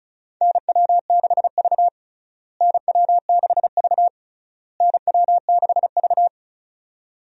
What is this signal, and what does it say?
Morse code, heard as a keyed tone, about 36 wpm, 700 Hz. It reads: NW6V NW6V NW6V
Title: USA Call Signs - 4 Characters - Repeated 3x - 35wpm